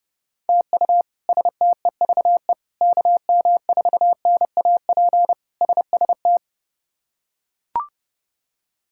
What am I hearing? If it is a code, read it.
Morse code, heard as a keyed tone, about 30 wpm, 700 Hz. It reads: TU STEVE KM4NAP SST
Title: K1USN SST QSO Practice - 30wpm